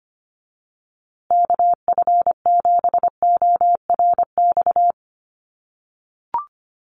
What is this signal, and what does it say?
Morse code, heard as a keyed tone, about 25 wpm, 700 Hz. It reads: KF7ORX
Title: USA Call Signs Repeated 3x 25wpm